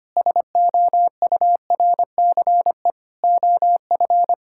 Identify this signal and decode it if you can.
Morse code, heard as a keyed tone, about 25 wpm, 700 Hz. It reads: SOURCE OF
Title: Sentences from Top 1000 Words - Review Part 5 - 25wpm